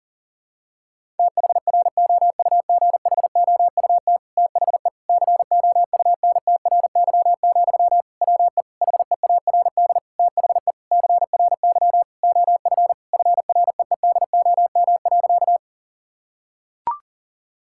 Morse code, heard as a keyed tone, about 37 wpm, 700 Hz. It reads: THROUGHOUT THE COUNTRY, WE HEARD THE CRY OF FREEDOM.
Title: Sentences from Top 1000 Words - Review Part 5 - 40wpm